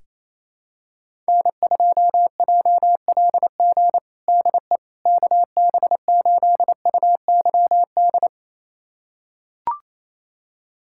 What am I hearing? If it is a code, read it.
Morse code, heard as a keyed tone, about 28 wpm, 700 Hz. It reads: N2JLG DE KB8UYD
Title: USA Callsign DE Callsign 28wpm